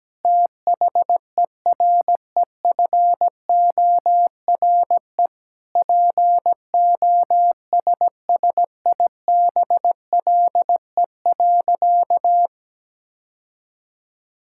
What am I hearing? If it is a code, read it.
Morse code, heard as a keyed tone, about 17 wpm, 700 Hz. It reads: THEREFORE POSSIBLE.
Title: Sentences from Top 1000 Words - Review Part 5 - 17wpm